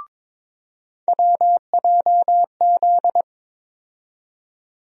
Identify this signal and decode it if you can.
Morse code, heard as a keyed tone, about 22 wpm, 700 Hz. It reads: WJZ
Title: USA Call Sign Postfixes - 3 Letters 22wpm